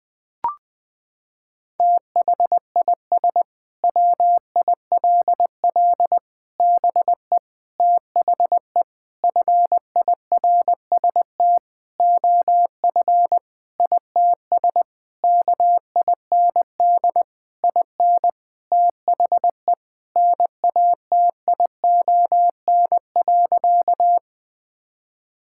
Morse code, heard as a keyed tone, about 20 wpm, 700 Hz. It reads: THIS WILL BE THE FIRST OF ITS KIND IN THE NATION.
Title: Sentences with 8-Word Phrases - 20wpm